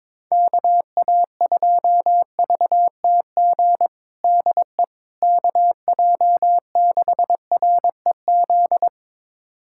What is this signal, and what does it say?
Morse code, heard as a keyed tone, about 22 wpm, 700 Hz. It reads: KA2VTG DE KJ6REZ
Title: USA Callsign DE Callsign - 6 Characters - 22wpm